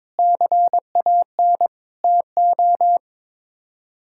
Morse code, heard as a keyed tone, about 22 wpm, 700 Hz. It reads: CAN TO
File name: practice / 22wpm / Sets of 8 Words - 22wpm